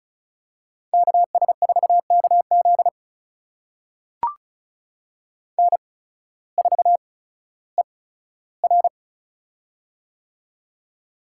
Morse code, heard as a keyed tone, about 34 wpm, spaced as for 16 wpm, 700 Hz. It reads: KS4KZ N4ER
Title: USA Call Signs - 8x Character Spacing - 35wpm